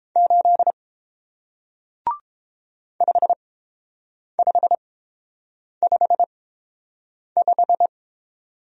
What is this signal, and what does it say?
Morse code, heard as a keyed tone, about 31 wpm, spaced as for 7 wpm, 700 Hz. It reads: 8 5555